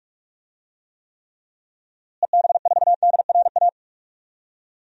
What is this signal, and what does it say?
Morse code, heard as a keyed tone, about 46 wpm, 700 Hz. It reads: EB4DRA
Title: DXCC-100 Call Signs - 6 Characters - Repeated 3x - 45wpm